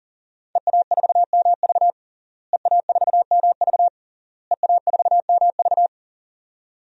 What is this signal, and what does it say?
Morse code, heard as a keyed tone, about 37 wpm, 700 Hz. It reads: EA4MV EA4MV EA4MV
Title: DXCC-100 Call Signs - 5 Characters - Repeated 3x - 40wpm